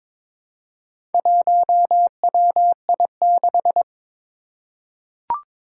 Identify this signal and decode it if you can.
Morse code, heard as a keyed tone, about 22 wpm, 700 Hz. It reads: 1WI6
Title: Four Challenging Letter-Number Combinations 22wpm